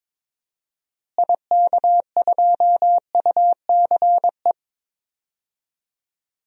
Morse code, heard as a keyed tone, about 22 wpm, 700 Hz. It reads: IK2UCE
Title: DXCC-100 Call Signs - 6 Characters - Repeated 3x - 22wpm